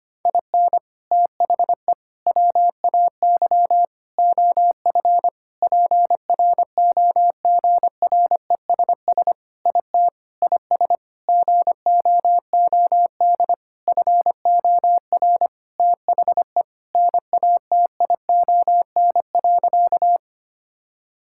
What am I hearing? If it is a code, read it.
Morse code, heard as a keyed tone, about 25 wpm, 700 Hz. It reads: IN THE WAY OF PROGRESS IT IS GOOD FOR THE NATION.
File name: practice / 25wpm / Sentences with 4-Word Phrases - 25wpm